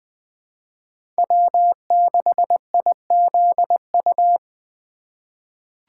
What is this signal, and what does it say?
Morse code, heard as a keyed tone, about 21 wpm, 700 Hz. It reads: W6IZU
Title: USA Call Signs - 4x Character Spacing - 20wpm